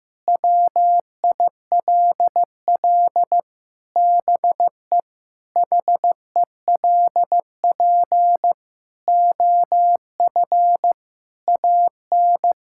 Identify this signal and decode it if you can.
Morse code, heard as a keyed tone, about 15 wpm, 700 Hz. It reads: WILL BE HELP OF AN
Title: Sentences from Top 1000 Words - Review Part 5 - 15wpm